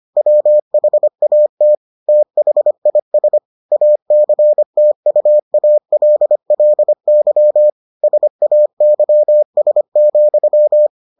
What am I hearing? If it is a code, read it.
Morse code, heard as a keyed tone, about 25 wpm, 600 Hz. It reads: WHAT THIS ACTUALLY SAYS,